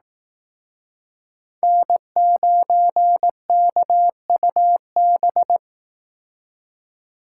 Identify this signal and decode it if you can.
Morse code, heard as a keyed tone, about 19 wpm, 700 Hz. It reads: N9KUB
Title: Speed Racing - USA 1x3 Call Signs 15wpm